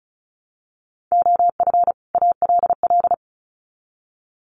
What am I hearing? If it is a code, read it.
Morse code, heard as a keyed tone, about 34 wpm, 700 Hz. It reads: OF ALL
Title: Sets of 2 Words 35wpm